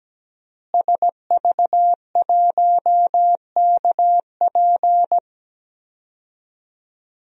Morse code, heard as a keyed tone, about 17 wpm, 700 Hz. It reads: SV1KP
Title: DXCC-100 Call Signs - 5 Characters - Repeated 2x - 17wpm